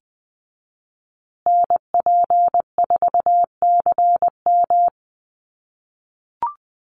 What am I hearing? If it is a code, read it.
Morse code, heard as a keyed tone, about 20 wpm, 700 Hz. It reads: NP4CM
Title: USA Call Signs Repeated 3x 20wpm